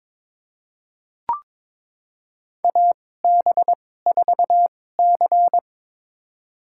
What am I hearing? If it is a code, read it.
Morse code, heard as a keyed tone, about 22 wpm, spaced as for 16 wpm, 700 Hz. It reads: AB4C
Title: USA Call Signs - 4 Characters - 2x Character Spacing - 22wpm